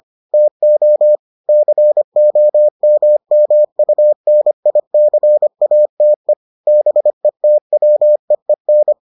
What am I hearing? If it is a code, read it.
Morse code, heard as a keyed tone, about 25 wpm, 600 Hz. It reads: TO COMMUNICATE BETWEEN